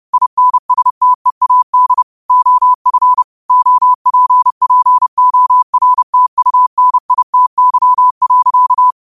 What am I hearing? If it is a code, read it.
Morse code, heard as a keyed tone, about 30 wpm, 1000 Hz. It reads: INSTEAD OF OPPORTUNITY.
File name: qotdfiles / 30wpm